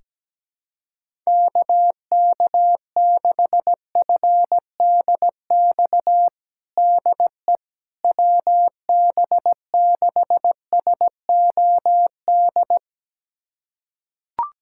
Morse code, heard as a keyed tone, about 17 wpm, 700 Hz. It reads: KK6FDX DE WB6SOD
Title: USA Callsign DE Callsign - 6 Characters - 17wpm